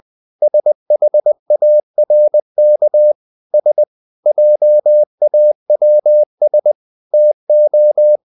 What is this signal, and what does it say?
Morse code, heard as a keyed tone, about 20 wpm, 600 Hz. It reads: SHARK S JAWS TO
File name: the old man and the sea part 2 20wpm